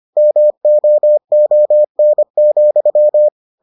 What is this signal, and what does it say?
Morse code, heard as a keyed tone, about 25 wpm, 600 Hz. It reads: MOON,